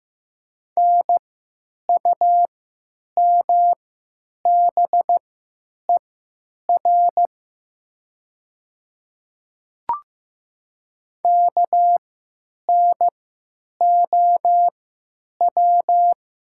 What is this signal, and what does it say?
Morse code, heard as a keyed tone, about 15 wpm, spaced as for 9 wpm, 700 Hz. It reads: NUMBER KNOW
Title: 3x Character Spacing - Top 200 Words - 15wpm